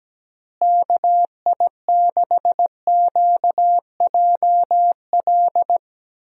Morse code, heard as a keyed tone, about 17 wpm, 700 Hz. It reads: KI6QJL
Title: USA Call Signs - 6 Characters - Repeated 2x - 17wpm